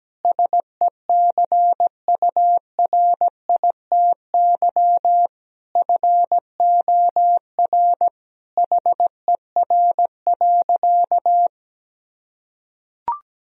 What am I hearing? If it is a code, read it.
Morse code, heard as a keyed tone, about 17 wpm, 700 Hz. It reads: SECURITY FOR HER.
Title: Sentences from Top 1000 Words - Review Part 4 - 17wpm